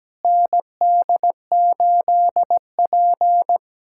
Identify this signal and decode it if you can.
Morse code, heard as a keyed tone, about 17 wpm, 700 Hz. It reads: ND8P
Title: USA Callsign DE Callsign - 4 Characters - 17wpm